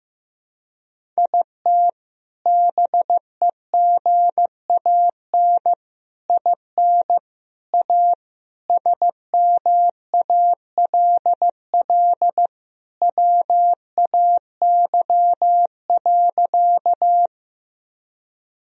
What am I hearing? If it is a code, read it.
Morse code, heard as a keyed tone, about 15 wpm, 700 Hz. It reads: IT BEGAN IN A SMALL WAY.
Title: Sentences from Top 1000 Words - Review Part 1 - 15wpm